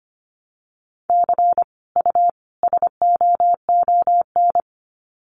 Morse code, heard as a keyed tone, about 25 wpm, 700 Hz. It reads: C U SOON
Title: Common QSO Phrases 25wpm